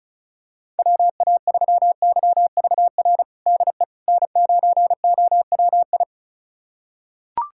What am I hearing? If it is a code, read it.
Morse code, heard as a keyed tone, about 36 wpm, 700 Hz. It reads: WA3YVR DE N9OWI
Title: USA Callsign DE Callsign 35wpm